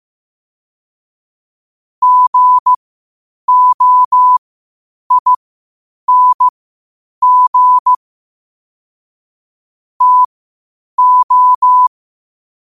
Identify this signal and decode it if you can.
Morse code, heard as a keyed tone, about 15 wpm, spaced as for 8 wpm, 1000 Hz. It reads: GOING TO
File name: qotdfiles / 8wpm